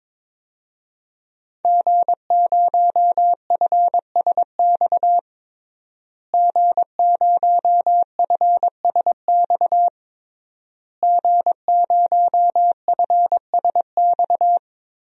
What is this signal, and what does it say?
Morse code, heard as a keyed tone, about 21 wpm, 700 Hz. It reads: G0FSX G0FSX G0FSX
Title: DXCC-100 Call Signs - 5 Characters - Repeated 3x - 22wpm